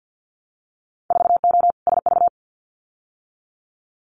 Morse code, heard as a keyed tone, about 50 wpm, 700 Hz. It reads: 4O SV
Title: Winter Field Day Exchanges 50wpm